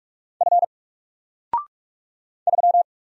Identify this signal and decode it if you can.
Morse code, heard as a keyed tone, about 45 wpm, 700 Hz. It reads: F 3
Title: Single Letter-Number 45wpm